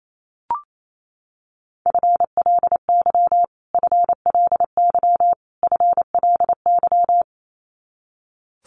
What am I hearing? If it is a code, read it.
Morse code, heard as a keyed tone, about 28 wpm, 700 Hz. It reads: FLY FLY FLY
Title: Top 500 Words Repeated 28wpm